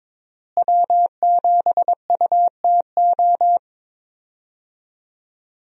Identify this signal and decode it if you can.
Morse code, heard as a keyed tone, about 22 wpm, 700 Hz. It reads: W7UTO
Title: USA Call Signs - 5 Characters - Repeated 3x - 22wpm